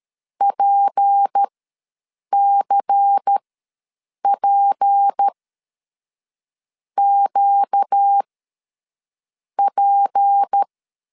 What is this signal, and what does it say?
Morse code, heard as a keyed tone, about 13 wpm, spaced as for 6 wpm, 800 Hz. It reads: PCPQP
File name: LEZIONI / 6 PKX0 / 6 P ws 12wpm 3117